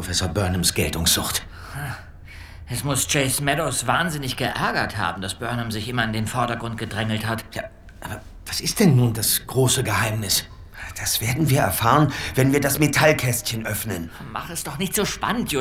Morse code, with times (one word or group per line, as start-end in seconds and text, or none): none